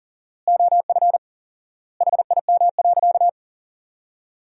Morse code, heard as a keyed tone, about 40 wpm, 700 Hz. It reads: OF HIM.
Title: 4x Word-Spacing - Sentences from Top 500 Words - 40wpm